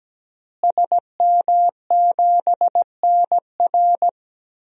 Morse code, heard as a keyed tone, about 17 wpm, 700 Hz. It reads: SM7NR
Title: DXCC-100 Call Signs - 5 Characters - Repeated 3x - 17wpm